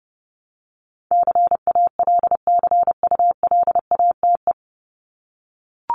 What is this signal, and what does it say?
Morse code, heard as a keyed tone, about 30 wpm, 700 Hz. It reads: CALCULATE
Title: Top 1751-2000 Words 30wpm